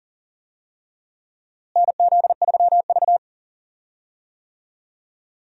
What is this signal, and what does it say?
Morse code, heard as a keyed tone, about 40 wpm, 700 Hz. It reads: NZ3V
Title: USA Call Signs - 4 Characters - Repeated 3x - 40wpm